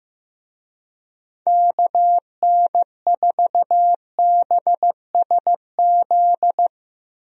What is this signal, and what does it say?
Morse code, heard as a keyed tone, about 15 wpm, 700 Hz. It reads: KN4BSZ
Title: USA Call Signs - 6 Characters - 15wpm